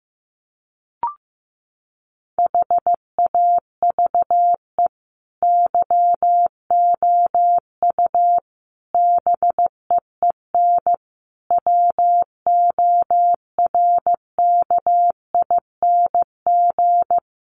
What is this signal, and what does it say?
Morse code, heard as a keyed tone, about 14 wpm, 700 Hz. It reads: HAVE YOU BEEN WORKING
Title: Sentences from Top 700 Words 15wpm